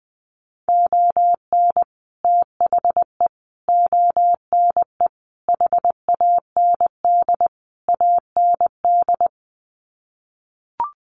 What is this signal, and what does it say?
Morse code, heard as a keyed tone, about 20 wpm, 700 Hz. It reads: ON THE ONE HAND AND
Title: Sets of 5 Words Encore2 20wpm